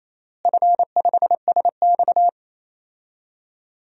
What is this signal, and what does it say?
Morse code, heard as a keyed tone, about 28 wpm, 700 Hz. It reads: F5SX
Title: DXCC-100 Call Signs - 4 Characters - Repeated 3x - 28wpm